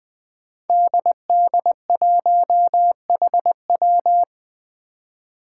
Morse code, heard as a keyed tone, about 20 wpm, 700 Hz. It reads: DD1HW
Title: Five Challenging Letter-Number Combinations 20wpm